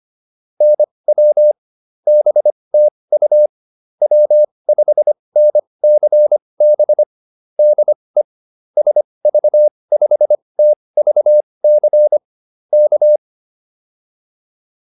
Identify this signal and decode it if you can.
Morse code, heard as a keyed tone, about 25 wpm, spaced as for 20 wpm, 600 Hz. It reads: NW BTU W5NCB DE SV5TVC K
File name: QSO408 20